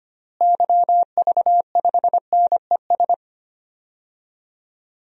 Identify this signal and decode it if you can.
Morse code, heard as a keyed tone, about 25 wpm, 700 Hz. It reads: YV5NES
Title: DXCC-100 Call Signs - 6 Characters - Repeated 2x - 25wpm